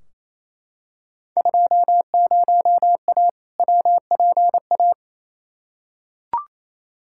Morse code, heard as a keyed tone, about 28 wpm, 700 Hz. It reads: 20A WPA
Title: Field Day Exchanges Encore - 28wpm